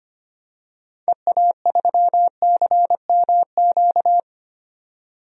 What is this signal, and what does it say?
Morse code, heard as a keyed tone, about 25 wpm, 700 Hz. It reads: EA3CMQ
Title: DXCC-100 Call Signs - 6 Characters - 25wpm